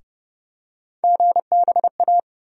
Morse code, heard as a keyed tone, about 30 wpm, 700 Hz. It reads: GBA
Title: USA Call Sign Postfixes - 3 Letters 30wpm